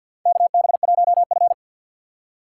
Morse code, heard as a keyed tone, about 53 wpm, 700 Hz. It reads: KB1F
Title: USA Call Signs - 4 Characters - 2x Character Spacing - 50wpm